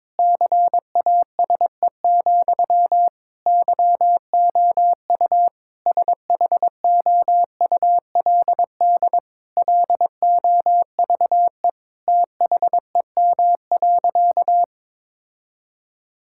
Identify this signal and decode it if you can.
Morse code, heard as a keyed tone, about 22 wpm, 700 Hz. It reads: CASE, YOU SHOULD LOVE THEM.
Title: Sentences from Top 1600 Words 22wpm